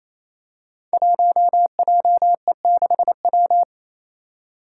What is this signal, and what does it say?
Morse code, heard as a keyed tone, about 28 wpm, 700 Hz. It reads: 1JE6W
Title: Five Challenging Letter-Number Combinations 28wpm